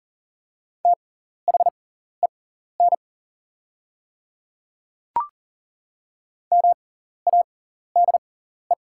Morse code, heard as a keyed tone, about 39 wpm, spaced as for 14 wpm, 690 Hz. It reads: THEN MADE
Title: 6x Character Spacing - Top 200 Words - 40wpm